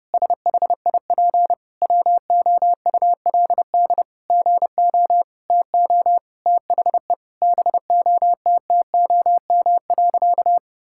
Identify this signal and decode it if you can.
Morse code, heard as a keyed tone, about 30 wpm, 700 Hz. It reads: SHIP WOULD GO TO THE BOTTOM.
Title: Sentences from Top 1000 Words - Review Part 6 - 30wpm